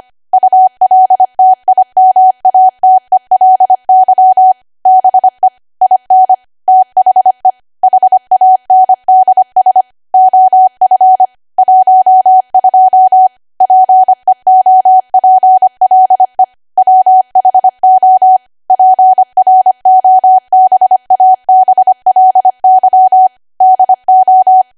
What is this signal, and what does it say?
Morse code, heard as a keyed tone, about 25 wpm, 750 Hz. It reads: ULTIMATELY BE IN THE HANDS OF 12 PEOPLE WHO PROBABLY DO
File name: W1AW-Level3/140218 25WPM